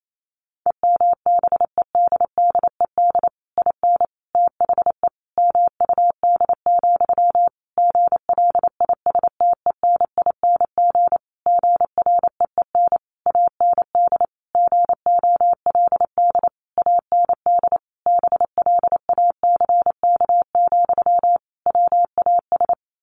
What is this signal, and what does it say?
Morse code, heard as a keyed tone, about 28 wpm, 700 Hz. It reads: EMBEDDED IN THE MUD, GLISTENING GREEN AND GOLD AND BLACK, WAS
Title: A Sound Of Thunder 28wpm